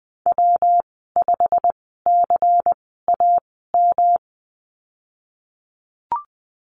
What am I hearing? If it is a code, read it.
Morse code, heard as a keyed tone, about 20 wpm, spaced as for 15 wpm, 700 Hz. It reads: W5CAM
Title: 140 Lesson 23g - Letter f - Call Signs 20wpm